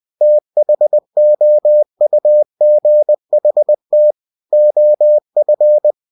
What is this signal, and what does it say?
Morse code, heard as a keyed tone, about 20 wpm, 600 Hz. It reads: THOUGHT OF
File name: cw-books / winnie-the-pooh / winnie-20wpm-page-0004